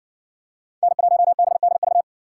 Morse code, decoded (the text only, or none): N1BDV